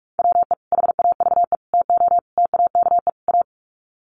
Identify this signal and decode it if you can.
Morse code, heard as a keyed tone, about 44 wpm, 700 Hz. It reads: WE HAVE TO TAKE A
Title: Sets of 5 Words Encore 45wpm